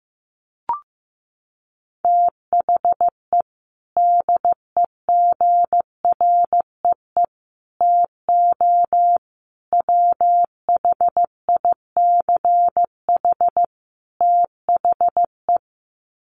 Morse code, heard as a keyed tone, about 15 wpm, 700 Hz. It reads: THE DEGREE TO WHICH THE
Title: Sets of 5 Words Encore2 15wpm